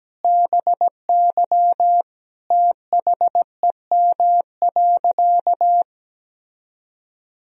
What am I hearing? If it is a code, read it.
Morse code, heard as a keyed tone, about 17 wpm, 700 Hz. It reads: BY THEM.
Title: Sentences from Top 1000 Words - Review Part 1 - 17wpm